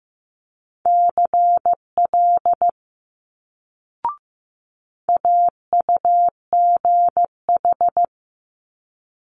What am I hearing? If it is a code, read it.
Morse code, heard as a keyed tone, about 15 wpm, 700 Hz. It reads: CL AUGH
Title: Multi-letter Phonemes Quick Fire 15wpm